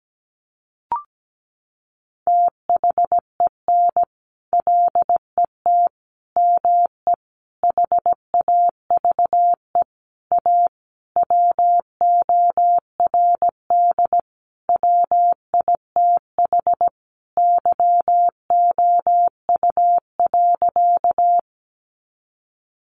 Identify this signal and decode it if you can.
Morse code, heard as a keyed tone, about 17 wpm, 700 Hz. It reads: THEN LET ME HAVE A WORD WITH YOU.
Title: Sentences from Top 300 Words 17wpm